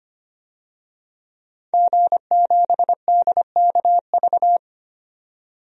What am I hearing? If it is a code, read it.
Morse code, heard as a keyed tone, about 25 wpm, 700 Hz. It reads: G7DKV
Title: DXCC-100 Call Signs - 5 Characters - 25wpm